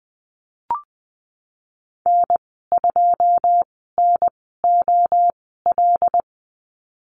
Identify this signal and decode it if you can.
Morse code, heard as a keyed tone, about 20 wpm, spaced as for 15 wpm, 700 Hz. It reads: N2NOL
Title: 082 Lesson 14g - Number 2 - Call Signs 20wpm